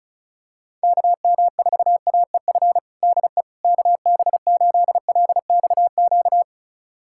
Morse code, heard as a keyed tone, about 36 wpm, 700 Hz. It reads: KM4AEF DE KB8LXQ
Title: USA Callsign DE Callsign - 6 Characters - 35wpm